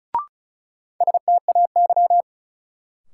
Morse code, heard as a keyed tone, about 36 wpm, 700 Hz. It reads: STAY
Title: Sentences from Top 500 Words - New Word Drill - 35wpm